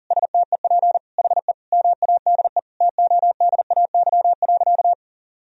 Morse code, heard as a keyed tone, about 40 wpm, 700 Hz. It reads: STEP HE MADE TODAY.
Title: Sentences from Top 1000 Words - Review Part 2 - 40wpm